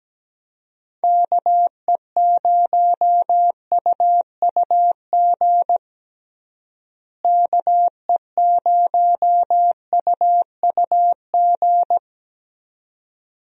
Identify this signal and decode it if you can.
Morse code, heard as a keyed tone, about 17 wpm, 700 Hz. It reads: KE0UUG KE0UUG
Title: USA Call Signs - 6 Characters - Repeated 2x - 17wpm